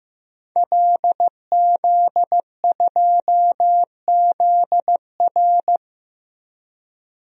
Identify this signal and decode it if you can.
Morse code, heard as a keyed tone, about 15 wpm, 700 Hz. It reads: LZ2ZR
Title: DXCC-100 Call Signs - 5 Characters - Repeated 3x - 15wpm